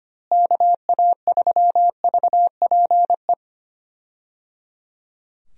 Morse code, heard as a keyed tone, about 25 wpm, 700 Hz. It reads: KA3VPE
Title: USA Call Signs - 6 Characters - Repeated 3x - 25wpm